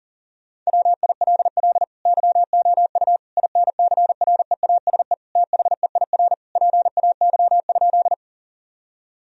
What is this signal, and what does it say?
Morse code, heard as a keyed tone, about 40 wpm, 700 Hz. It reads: WILL YOU INCREASE THEIR PAY?
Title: Sentences from Top 1000 Words - Review Part 4 - 40wpm